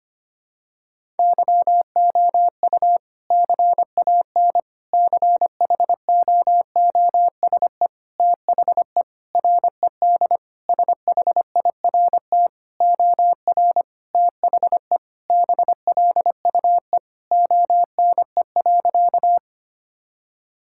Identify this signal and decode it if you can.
Morse code, heard as a keyed tone, about 25 wpm, 700 Hz. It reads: YOU CAN CHOOSE THE RED SHIRT OR THE BLUE ONE.
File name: practice / 25wpm / Sentences with 2-Word Phrases - 25wpm